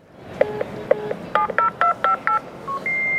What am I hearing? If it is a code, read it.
DTMF, heard as telephone touch tones, tones of about 110 ms. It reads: 70258